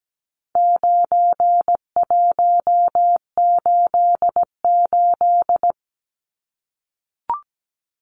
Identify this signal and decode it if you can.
Morse code, heard as a keyed tone, about 17 wpm, 700 Hz. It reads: 9188
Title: Numbers - 4-Digit 17wpm